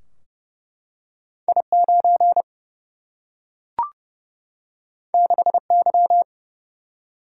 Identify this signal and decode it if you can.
Morse code, heard as a keyed tone, about 30 wpm, 700 Hz. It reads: I9 6Y
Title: Single-Letter Single-Number 30wpm